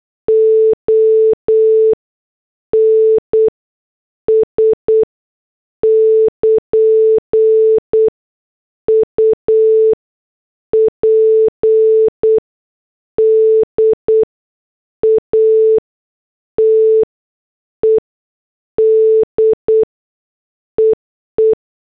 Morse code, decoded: ONS <KN> UPDATED EE